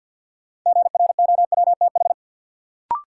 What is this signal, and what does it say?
Morse code, heard as a keyed tone, about 52 wpm, 700 Hz. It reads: GROWTH
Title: Sentences from Top 1300 Words - New Word Drill Alt - 50wpm